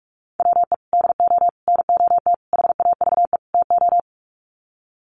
Morse code, heard as a keyed tone, about 44 wpm, 700 Hz. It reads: WE DO NOT HAVE TO
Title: Sets of 5 Words 45wpm